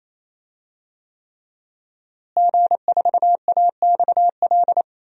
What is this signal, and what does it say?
Morse code, heard as a keyed tone, about 28 wpm, 700 Hz. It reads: G4AXL
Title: DXCC-100 Call Signs - 5 Characters - Repeated 3x - 28wpm